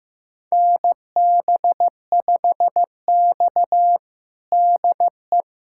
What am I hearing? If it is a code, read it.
Morse code, heard as a keyed tone, about 15 wpm, 700 Hz. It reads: NB5X DE
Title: USA Callsign DE Callsign - 4 Characters - 15wpm